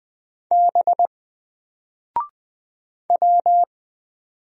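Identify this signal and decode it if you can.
Morse code, heard as a keyed tone, about 20 wpm, 700 Hz. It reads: B W
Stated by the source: Single letter and number Morse drill at 20 wpm, from N4LCD.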